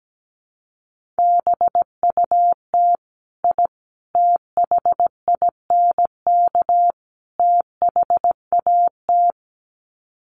Morse code, heard as a keyed tone, about 17 wpm, 700 Hz. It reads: BUT I THINK THAT
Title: Sets of 4 Words 17wpm